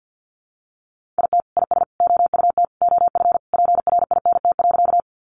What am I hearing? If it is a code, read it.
Morse code, heard as a keyed tone, about 50 wpm, 700 Hz. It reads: IT IS OUT OF PRINT.
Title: Sentences from Top 1400 Words 50wpm